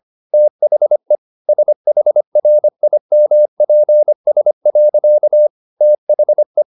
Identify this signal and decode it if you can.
Morse code, heard as a keyed tone, about 25 wpm, 600 Hz. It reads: THE SHRIMPS. THE